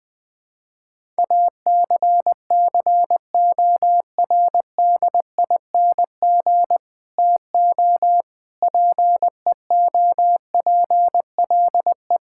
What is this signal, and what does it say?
Morse code, heard as a keyed tone, about 20 wpm, 700 Hz. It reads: ACCORDING TO PEOPLE
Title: Sentences with 7-Word Phrases - 20wpm